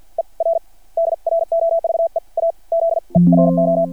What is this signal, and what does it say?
Morse code, heard as a keyed tone, about 53 wpm, spaced as for 42 wpm, 660 Hz. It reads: ER DROVE A ZERO
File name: archived / CRSnet